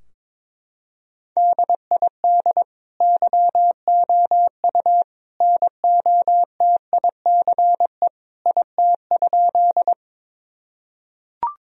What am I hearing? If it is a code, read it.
Morse code, heard as a keyed tone, about 22 wpm, 700 Hz. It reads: DID YOU NOTICE IT?
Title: Sentences from Top 1000 Words - Review Part 2 - 22wpm